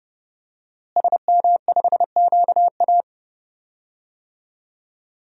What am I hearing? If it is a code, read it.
Morse code, heard as a keyed tone, about 30 wpm, 700 Hz. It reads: SM5QA
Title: DXCC-100 Call Signs - 5 Characters - Repeated 2x - 30wpm